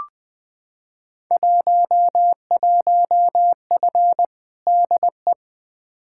Morse code, heard as a keyed tone, about 20 wpm, 700 Hz. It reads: 11F DE
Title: Field Day Exchanges Encore - 20wpm